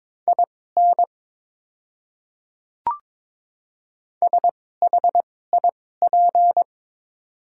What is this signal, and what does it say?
Morse code, heard as a keyed tone, about 22 wpm, spaced as for 16 wpm, 700 Hz. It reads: IN SHIP
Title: 2x Character Spacing - Top 500 Words - 22wpm